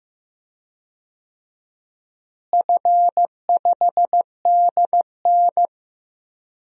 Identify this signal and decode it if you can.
Morse code, heard as a keyed tone, about 15 wpm, 700 Hz. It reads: F5DN
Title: DXCC-100 Call Signs - 4 Characters - Repeated 2x - 15wpm